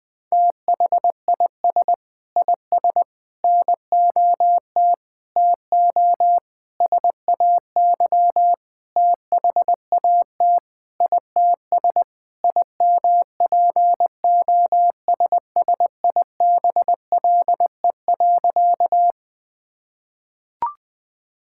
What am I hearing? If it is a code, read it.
Morse code, heard as a keyed tone, about 20 wpm, 700 Hz. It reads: THIS IS NOT TO SAY THAT ITS IMPOSSIBLE.
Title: Sentences with 6-Word Phrases - Spoken Beginning - 20wpm